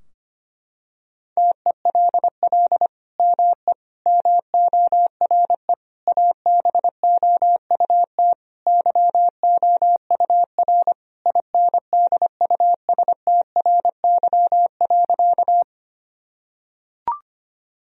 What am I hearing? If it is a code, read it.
Morse code, heard as a keyed tone, about 25 wpm, 700 Hz. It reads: TELL ME MORE ABOUT YOUR INDUSTRY.
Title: Sentences from Top 1000 Words - Review Part 4 - 25wpm